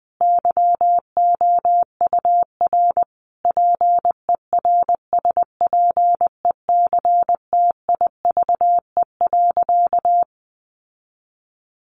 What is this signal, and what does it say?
Morse code, heard as a keyed tone, about 20 wpm, 700 Hz. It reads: YOUR PERSPECTIVE.
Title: Sentences from Top 1200 Words 20wpm